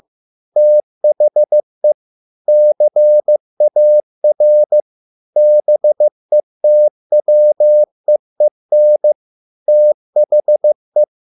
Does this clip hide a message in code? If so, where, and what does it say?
Yes: Morse code from 0.6 to 11.0 s, heard as a keyed tone, about 15 wpm, 600 Hz. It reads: THE CAR BETWEEN THE